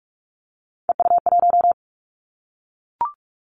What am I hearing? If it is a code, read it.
Morse code, heard as a keyed tone, about 44 wpm, 700 Hz. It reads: EU1
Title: Callsign Prefixes for Top 100 DXCC - 45wpm